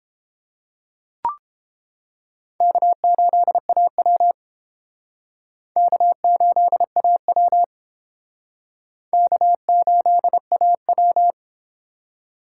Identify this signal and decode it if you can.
Morse code, heard as a keyed tone, about 30 wpm, 700 Hz. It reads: K8AW K8AW K8AW